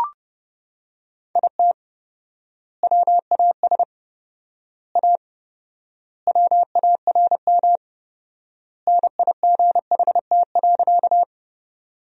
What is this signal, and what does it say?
Morse code, heard as a keyed tone, about 28 wpm, 700 Hz. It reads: IT WAS A WARM NIGHT.
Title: 4x Word-Spacing - Sentences from Top 500 Words - 30wpm